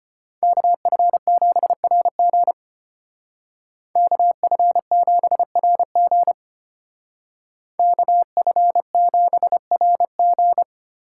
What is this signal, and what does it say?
Morse code, heard as a keyed tone, about 30 wpm, 700 Hz. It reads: KF7RG KF7RG KF7RG
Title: Speed Racing - USA 2x2 Call Signs 25wpm